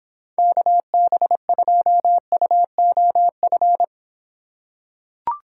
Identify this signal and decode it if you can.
Morse code, heard as a keyed tone, about 24 wpm, 700 Hz. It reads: KB2UOF